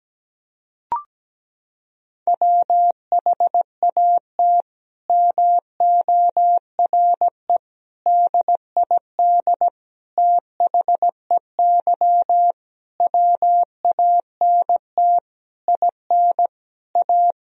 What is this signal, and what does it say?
Morse code, heard as a keyed tone, about 17 wpm, 700 Hz. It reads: WHAT MORE DID THEY WANT IN A
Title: Sentences from Top 1000 Words - Review Part 3 - 17wpm